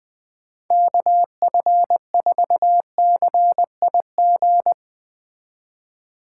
Morse code, heard as a keyed tone, about 20 wpm, 700 Hz. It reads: KF4CIG
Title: Speed Racing - USA 2x3 Call Signs 17wpm